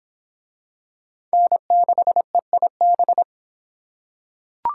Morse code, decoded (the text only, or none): N6EIB